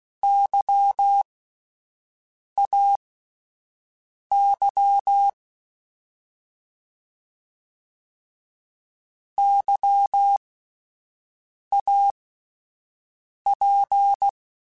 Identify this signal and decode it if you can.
Morse code, heard as a keyed tone, about 16 wpm, spaced as for 5 wpm, 800 Hz. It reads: YAY YAP
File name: Alphabetical / Alphabetical mp3s / Ys @5